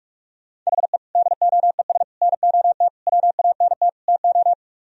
Morse code, heard as a keyed tone, about 46 wpm, 700 Hz. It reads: HE DOES NOT WANT TO